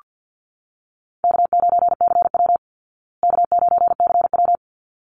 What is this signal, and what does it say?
Morse code, heard as a keyed tone, about 48 wpm, 700 Hz. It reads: K9CW K9CW
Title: Easy Call Signs Repeated 2x - 50wpm